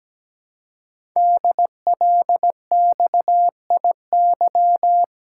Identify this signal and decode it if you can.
Morse code, heard as a keyed tone, about 17 wpm, 700 Hz. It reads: DLXIY
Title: Code Groups - 5 Letters - 17wpm